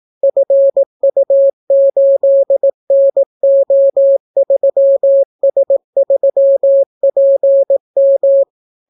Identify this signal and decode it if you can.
Morse code, heard as a keyed tone, about 18 wpm, 550 Hz. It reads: FU8NO3S3PM